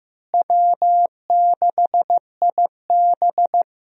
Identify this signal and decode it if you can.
Morse code, heard as a keyed tone, about 15 wpm, 700 Hz. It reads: W6IB
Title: USA Call Signs - 4 Characters - Repeated 3x - 15wpm